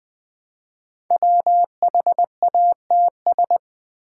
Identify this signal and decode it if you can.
Morse code, heard as a keyed tone, about 20 wpm, 700 Hz. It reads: WHATS
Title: Build A Word - 5 Letters At A Time - 10-Letter Words - 20wpm